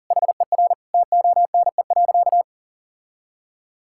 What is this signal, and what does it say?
Morse code, heard as a keyed tone, about 40 wpm, 700 Hz. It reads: HER TONE.
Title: Sentences from Top 1600 Words 40wpm